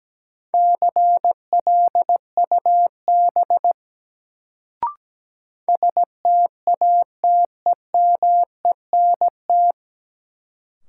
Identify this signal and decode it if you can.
Morse code, heard as a keyed tone, about 17 wpm, 700 Hz. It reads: CLUB STATEMENT
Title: Sentences from Top 1200 Words - New Word Drill - 17wpm